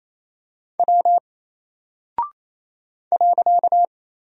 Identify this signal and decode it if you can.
Morse code, heard as a keyed tone, about 27 wpm, 700 Hz. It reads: W .
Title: Single Character (Letter:Number:Punctuation:Prosign) 28wpm